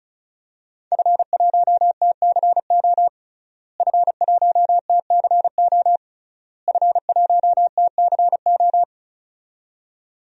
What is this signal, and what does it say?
Morse code, heard as a keyed tone, about 34 wpm, 700 Hz. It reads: F1TCO F1TCO F1TCO
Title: DXCC-100 Call Signs - 5 Characters - Repeated 3x - 35wpm